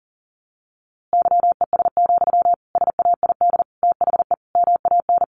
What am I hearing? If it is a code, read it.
Morse code, heard as a keyed tone, about 40 wpm, 700 Hz. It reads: YES, SAID THE MAN
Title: A Sound Of Thunder 40wpm